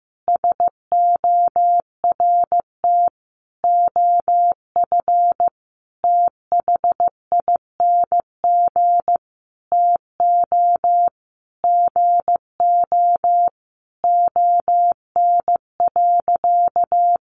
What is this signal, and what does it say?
Morse code, heard as a keyed tone, about 15 wpm, 700 Hz. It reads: SORT OF THING TO GO ON.
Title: Sentences from Top 1300 Words 15wpm